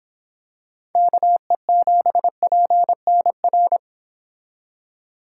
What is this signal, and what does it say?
Morse code, heard as a keyed tone, about 26 wpm, 700 Hz. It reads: KE7PNR